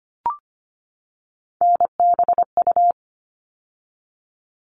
Morse code, heard as a keyed tone, about 25 wpm, 700 Hz. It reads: NBU
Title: All Cut Numbers - 3-Digit 25wpm